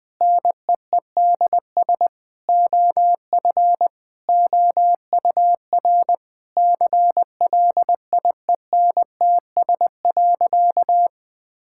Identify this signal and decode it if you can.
Morse code, heard as a keyed tone, about 20 wpm, 700 Hz. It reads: NEEDS OF OUR CLIENTS.
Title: Sentences with 5-Word Phrases - 20wpm